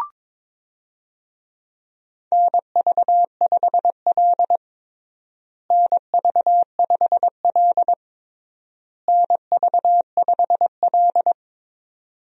Morse code, heard as a keyed tone, about 22 wpm, 700 Hz. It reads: NV5L NV5L NV5L
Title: USA Call Signs - 4 Characters - Repeated 3x - 22wpm